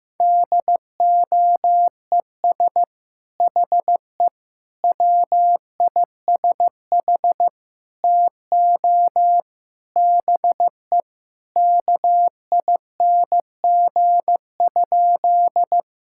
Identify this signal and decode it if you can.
Morse code, heard as a keyed tone, about 15 wpm, 700 Hz. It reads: DOES HE WISH TO BE KING?
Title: Sentences from Top 1600 Words 15wpm